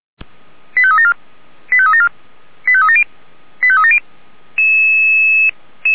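Five-tone selective call (ZVEI), heard as quick runs of five tones, tones of about 70 ms. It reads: 85374 85374 85370 85370
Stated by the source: Five-tone ZVEI selective call 89233 received by SDR, served by Feuerwehr Orlamünde.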